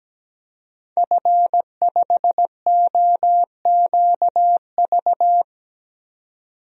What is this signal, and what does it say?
Morse code, heard as a keyed tone, about 17 wpm, 700 Hz. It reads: F5OQV
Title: DXCC-100 Call Signs - 5 Characters - 17wpm